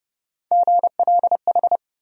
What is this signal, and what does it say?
Morse code, heard as a keyed tone, about 32 wpm, 700 Hz. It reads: GLH
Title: USA Call Sign Postfixes - 3 Letters 30wpm